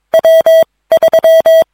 Morse code, heard as a keyed tone, about 22 wpm, spaced as for 17 wpm, 650 Hz. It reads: W3